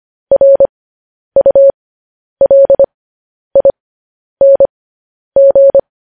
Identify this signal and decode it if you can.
Morse code, heard as a keyed tone, about 25 wpm, spaced as for 10 wpm, 550 Hz. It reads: RULING
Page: News Headlines in Morse Code at 10 WPM